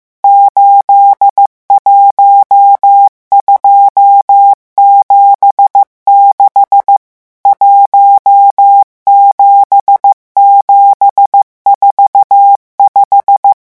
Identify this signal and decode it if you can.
Morse code, heard as a keyed tone, about 15 wpm, 800 Hz. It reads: 81276 17745
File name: morse mp3 / revision11 chiffres vitesse 14